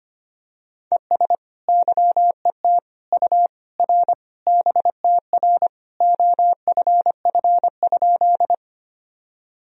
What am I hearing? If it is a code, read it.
Morse code, heard as a keyed tone, about 25 wpm, 700 Hz. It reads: ES YET U R BTR OFF?
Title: Sentences from Top 500 Words and Abbreviations 25wpm